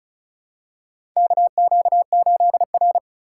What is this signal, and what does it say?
Morse code, heard as a keyed tone, about 34 wpm, 700 Hz. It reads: KQ8R
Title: USA Call Signs - 2x Character Spacing - 35wpm